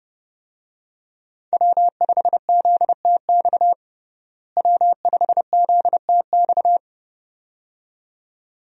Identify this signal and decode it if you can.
Morse code, heard as a keyed tone, about 30 wpm, 700 Hz. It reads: W5ZTX W5ZTX
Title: USA Call Signs - 5 Characters - Repeated 2x - 30wpm